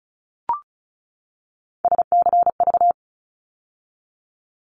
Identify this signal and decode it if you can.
Morse code, heard as a keyed tone, about 34 wpm, 700 Hz. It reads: SCV
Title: ARRL Sections 35wpm